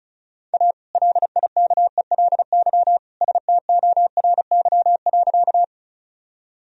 Morse code, heard as a keyed tone, about 34 wpm, 700 Hz. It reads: A LIKELY STORY.
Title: Sentences from Top 1000 Words - Review Part 4 - 35wpm